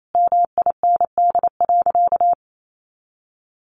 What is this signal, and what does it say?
Morse code, heard as a keyed tone, about 28 wpm, 700 Hz. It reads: MIND.
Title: Sentences from Top 600 Words 28wpm